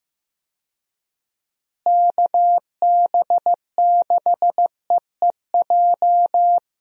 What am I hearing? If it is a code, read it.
Morse code, heard as a keyed tone, about 16 wpm, 700 Hz. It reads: KB6EEJ
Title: USA Call Signs - 6 Characters - Repeated 2x - 15wpm